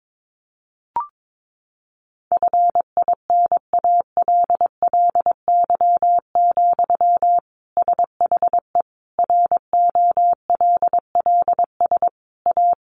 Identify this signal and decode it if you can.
Morse code, heard as a keyed tone, about 22 wpm, 700 Hz. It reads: FINALLY, SHE ROLLS A
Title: Easy Phrases 22wpm